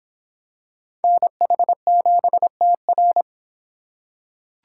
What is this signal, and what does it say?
Morse code, heard as a keyed tone, about 24 wpm, 700 Hz. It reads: NH7TR